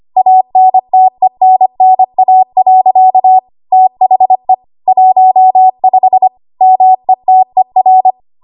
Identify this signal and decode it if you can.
Morse code, heard as a keyed tone, about 25 wpm, 750 Hz. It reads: ANTENNA. THE 15 METER
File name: Archive / 25 WPM / 25WPM